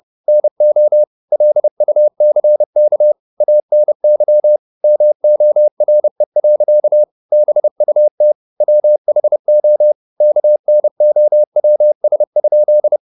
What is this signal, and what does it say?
Morse code, heard as a keyed tone, about 30 wpm, 600 Hz. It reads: NO LUCK ANY MORE. BUT WHO KNOWS?
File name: the old man and the sea / the old man and the sea 1 30wpm